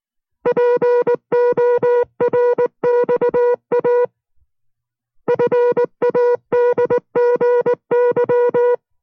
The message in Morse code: PORXA FADGY